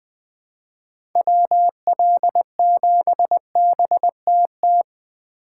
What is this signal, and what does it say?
Morse code, heard as a keyed tone, about 20 wpm, 700 Hz. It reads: WL7BTT
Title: USA Call Signs - 4x Character Spacing - 20wpm